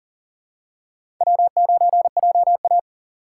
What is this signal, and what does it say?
Morse code, heard as a keyed tone, about 38 wpm, 700 Hz. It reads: W9JA
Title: USA Call Signs - 2x Character Spacing - 40wpm